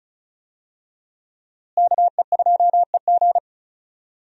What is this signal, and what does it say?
Morse code, heard as a keyed tone, about 36 wpm, 700 Hz. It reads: KE2EG
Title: USA Call Signs - 5 Characters - Repeated 2x - 35wpm